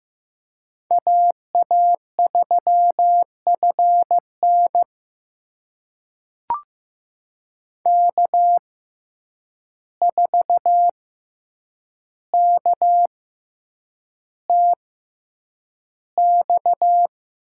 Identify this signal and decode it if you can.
Morse code, heard as a keyed tone, about 15 wpm, spaced as for 8 wpm, 700 Hz. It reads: AA3FN K4KTX
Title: USA Call Signs - 5 Characters - 6x Character Spacing - 15wpm